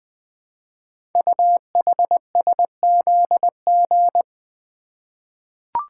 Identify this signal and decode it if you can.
Morse code, heard as a keyed tone, about 20 wpm, 700 Hz. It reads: UHSZG
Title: Code Groups - 5 Letters - 20wpm